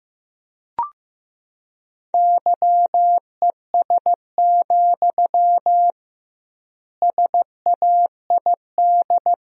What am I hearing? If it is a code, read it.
Morse code, heard as a keyed tone, about 16 wpm, 700 Hz. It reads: YES, SAID
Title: 2x Word-Spacing - Sentences from Top 500 Words - 15wpm